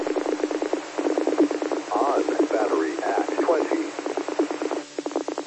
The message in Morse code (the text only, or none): none